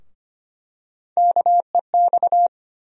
Morse code, heard as a keyed tone, about 25 wpm, 700 Hz. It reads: KEX
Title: USA Call Sign Postfixes - 3 Letters 25wpm